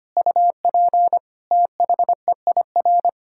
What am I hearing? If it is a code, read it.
Morse code, heard as a keyed tone, about 25 wpm, 700 Hz. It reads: UP THEIR
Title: Pangram Challenge v2 - 25wpm